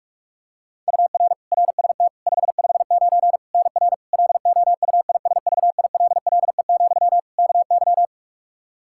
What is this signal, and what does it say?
Morse code, heard as a keyed tone, about 46 wpm, 700 Hz. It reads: UR RST 559 NR LOUISVILLE, KY
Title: K1USN SST QSO Practice 45wpm